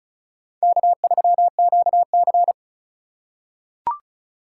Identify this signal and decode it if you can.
Morse code, heard as a keyed tone, about 36 wpm, 700 Hz. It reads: K3QC
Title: USA Call Signs - 2x Character Spacing - 35wpm